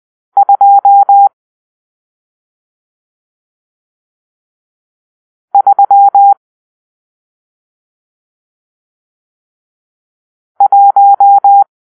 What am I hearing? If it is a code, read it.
Morse code, heard as a keyed tone, about 20 wpm, 800 Hz. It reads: 2 3 1
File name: arrl / Grp-6 Drill-1